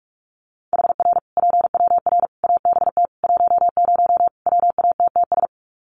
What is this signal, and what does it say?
Morse code, heard as a keyed tone, about 44 wpm, 700 Hz. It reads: HR PWR ABT 10 WATTS
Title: Common QSO Phrases 45wpm